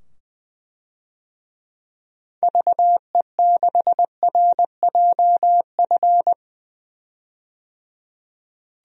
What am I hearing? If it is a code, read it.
Morse code, heard as a keyed tone, about 20 wpm, 700 Hz. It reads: VE6RJF
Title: DXCC-100 Call Signs - 6 Characters - 20wpm